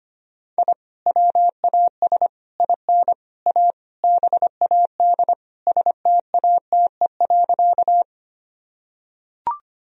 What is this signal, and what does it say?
Morse code, heard as a keyed tone, about 25 wpm, 700 Hz. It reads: I WAS IN A BAD STATE.
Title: Sentences from Top 1000 Words - Review Part 2 - 25wpm